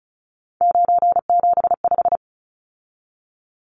Morse code, heard as a keyed tone, about 34 wpm, 700 Hz. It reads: 975
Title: Numbers - 3-Digit 35wpm